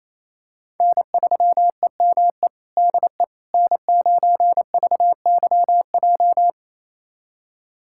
Morse code, heard as a keyed tone, about 28 wpm, 700 Hz. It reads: N3EME DE N9VYJ
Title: USA Callsign DE Callsign - 5 Characters - 28wpm